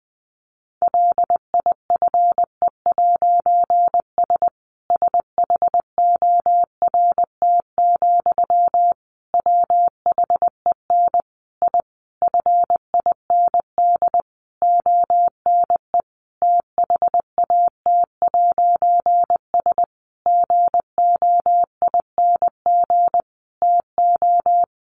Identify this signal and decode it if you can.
Morse code, heard as a keyed tone, about 20 wpm, 700 Hz. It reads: LIFE'S SHORT, WHEN I FIND ONE THAT'S GOING TO
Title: A Sound Of Thunder 20wpm